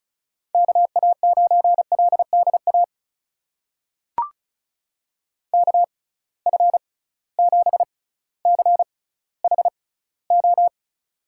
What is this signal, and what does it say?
Morse code, heard as a keyed tone, about 36 wpm, spaced as for 18 wpm, 700 Hz. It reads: KA9LDA KF7CHO
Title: USA Call Signs - 6 Characters - 6x Character Spacing - 35wpm